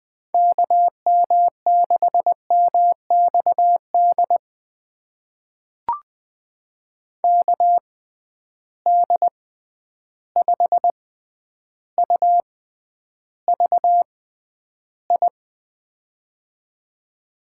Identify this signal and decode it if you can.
Morse code, heard as a keyed tone, about 20 wpm, spaced as for 10 wpm, 700 Hz. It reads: KM6MXD KD5UVI
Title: USA Call Signs - 6 Characters - 6x Character Spacing - 20wpm